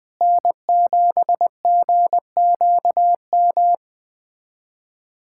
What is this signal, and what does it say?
Morse code, heard as a keyed tone, about 19 wpm, 700 Hz. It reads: N7GQM
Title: USA Callsign DE Callsign 20wpm